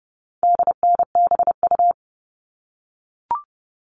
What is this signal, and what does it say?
Morse code, heard as a keyed tone, about 30 wpm, 700 Hz. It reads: DNBU
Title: All Cut Numbers - 4-Digit 30wpm